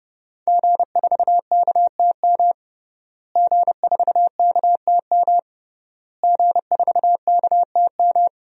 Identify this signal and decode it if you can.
Morse code, heard as a keyed tone, about 32 wpm, 700 Hz. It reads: G4KTM G4KTM G4KTM
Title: DXCC-100 Call Signs - 5 Characters - Repeated 3x - 30wpm